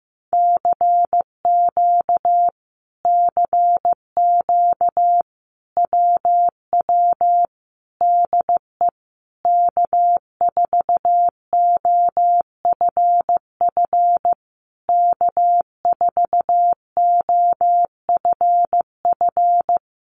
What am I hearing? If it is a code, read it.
Morse code, heard as a keyed tone, about 15 wpm, 700 Hz. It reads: CQ CQ WW DE K4OFF K4OFF
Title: Common QSO Phrases Encore 15wpm